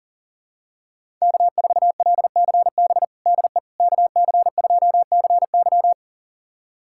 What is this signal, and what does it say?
Morse code, heard as a keyed tone, about 40 wpm, 700 Hz. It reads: K4LCB DE KC2CY
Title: USA Callsign DE Callsign - 5 Characters - 40wpm